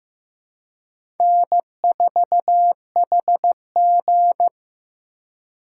Morse code, heard as a keyed tone, about 15 wpm, 700 Hz. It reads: N4HG
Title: USA Call Signs - 4 Characters - 2x Character Spacing - 15wpm